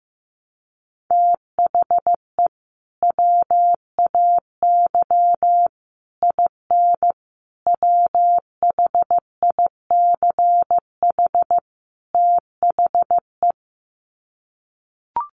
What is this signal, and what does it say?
Morse code, heard as a keyed tone, about 15 wpm, 700 Hz. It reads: THE WAY IN WHICH THE
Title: Sets of 5 Words Encore2 15wpm